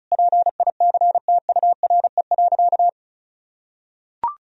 Morse code, heard as a keyed tone, about 34 wpm, 700 Hz. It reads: PICTURE.
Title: Sentences from Top 1000 Words - Review Part 1 - 35wpm